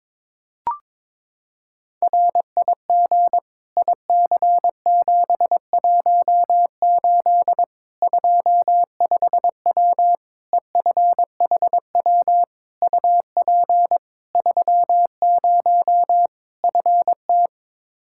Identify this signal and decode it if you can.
Morse code, heard as a keyed tone, about 22 wpm, 700 Hz. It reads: RIG IC718 25W EFHW UP 30 FT
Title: Practice QSO - Exchange 2 - Info - 22wpm